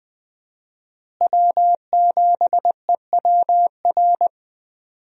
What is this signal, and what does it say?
Morse code, heard as a keyed tone, about 21 wpm, 700 Hz. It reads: W7EWR
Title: USA Call Signs - 5 Characters - Repeated 2x - 20wpm